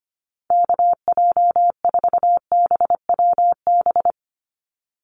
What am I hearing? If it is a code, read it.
Morse code, heard as a keyed tone, about 25 wpm, 700 Hz. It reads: KJ4BWB
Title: USA Call Signs 25wpm